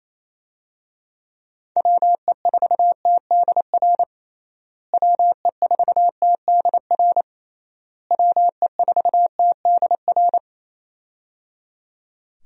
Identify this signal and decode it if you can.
Morse code, heard as a keyed tone, about 28 wpm, 700 Hz. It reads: WE4TDR WE4TDR WE4TDR
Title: USA Call Signs - 6 Characters - Repeated 3x - 28wpm